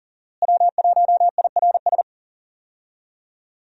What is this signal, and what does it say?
Morse code, heard as a keyed tone, about 40 wpm, 700 Hz. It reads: W1IRS
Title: USA Call Signs - 5 Characters - Repeated 2x - 40wpm